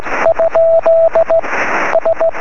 Morse code, heard as a keyed tone, about 17 wpm, 650 Hz, with radio noise. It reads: ? S
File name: sm7lcb / 20 23cm / 1544 pa3bfk 23cm